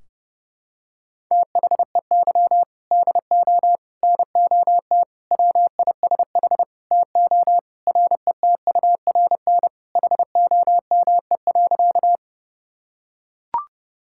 Morse code, heard as a keyed tone, about 32 wpm, 700 Hz. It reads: THEY DO NOT WISH TO RETURN HOME.
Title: Sentences from Top 1600 Words 30wpm